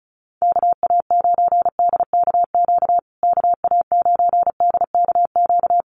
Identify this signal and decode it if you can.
Morse code, heard as a keyed tone, about 34 wpm, 700 Hz. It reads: KA9DKQ KA9DKQ
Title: USA Call Signs Repeated 3x 35wpm